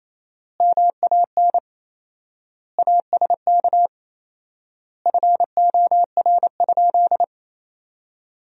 Morse code, heard as a keyed tone, about 28 wpm, 700 Hz. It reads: MAN ASK FOR?
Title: 4x Word-Spacing - Sentences from Top 200 Words - 28wpm